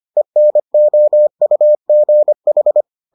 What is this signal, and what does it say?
Morse code, heard as a keyed tone, about 25 wpm, 600 Hz. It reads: ENOUGH